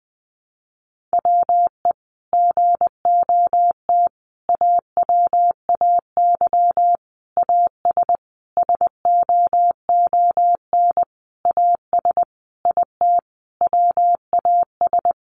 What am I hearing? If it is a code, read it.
Morse code, heard as a keyed tone, about 20 wpm, 700 Hz. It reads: WE GOT AWAY AS SOON AS IT WAS
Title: Sentences from Top 600 Words 20wpm